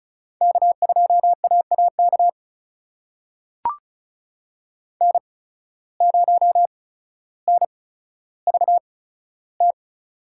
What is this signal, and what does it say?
Morse code, heard as a keyed tone, about 36 wpm, spaced as for 15 wpm, 700 Hz. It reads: K2AAK N0NVT
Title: USA Call Signs - 5 Characters - 8x Character Spacing - 35wpm